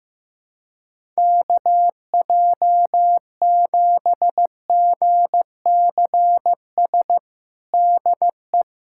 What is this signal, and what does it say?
Morse code, heard as a keyed tone, about 15 wpm, 700 Hz. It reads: KJ7GCS DE
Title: USA Callsign DE Callsign 15wpm